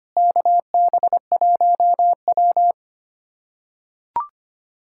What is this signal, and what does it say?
Morse code, heard as a keyed tone, about 25 wpm, 700 Hz. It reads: KB1W
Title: K1USN SST QSO Practice - 25wpm